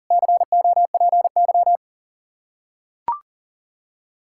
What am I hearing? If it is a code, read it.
Morse code, heard as a keyed tone, about 40 wpm, 700 Hz. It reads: COPY
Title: Practice QSO - Exchange 3 - More Info - 40wpm